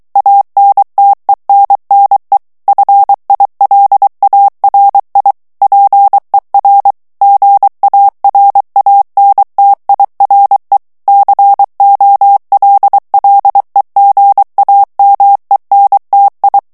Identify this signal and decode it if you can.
Morse code, heard as a keyed tone, about 23 wpm, 800 Hz. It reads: ANTENNE FILARI PER GARANTIRE COLLEGAMENTI
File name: morse mp3 / ITALIEN texte clair 1 vitesse 22 mots minute